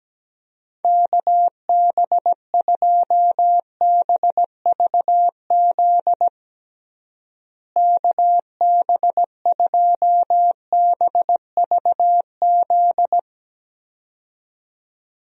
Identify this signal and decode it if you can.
Morse code, heard as a keyed tone, about 17 wpm, 700 Hz. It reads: KB2BVZ KB2BVZ
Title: USA Call Signs - 6 Characters - Repeated 2x - 17wpm